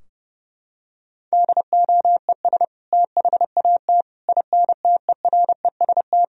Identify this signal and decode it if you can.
Morse code, heard as a keyed tone, about 28 wpm, 700 Hz. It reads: DOES THAT INTEREST
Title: Sentences from Top 1000 Words - Review Part 2 - 30wpm